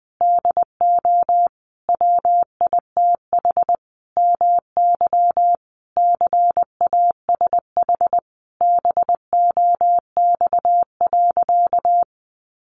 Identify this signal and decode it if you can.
Morse code, heard as a keyed tone, about 20 wpm, 700 Hz. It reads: DO WITH MY CASH BOX.
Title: Sentences from Top 700 Words 20wpm